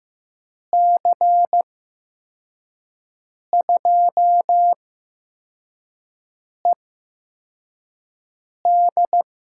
Morse code, heard as a keyed tone, about 15 wpm, spaced as for 4 wpm, 700 Hz. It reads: C2ED
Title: USA Call Signs - 6 Characters - 8x Character Spacing - 15wpm